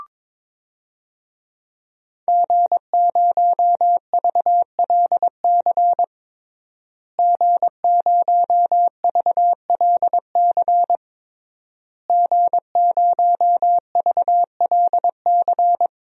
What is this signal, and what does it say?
Morse code, heard as a keyed tone, about 22 wpm, 700 Hz. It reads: G0VLC G0VLC G0VLC
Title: DXCC-100 Call Signs - 5 Characters - Repeated 3x - 22wpm